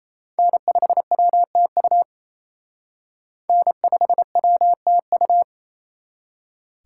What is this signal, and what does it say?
Morse code, heard as a keyed tone, about 34 wpm, 700 Hz. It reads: N5WTU N5WTU
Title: Speed Racing - USA 1x3 Call Signs 28wpm